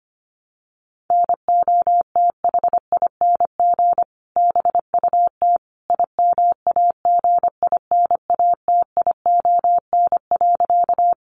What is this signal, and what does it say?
Morse code, heard as a keyed tone, about 25 wpm, 700 Hz. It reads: NOTHING BUT IMAGINATION.
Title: Sentences from Top 1300 Words 25wpm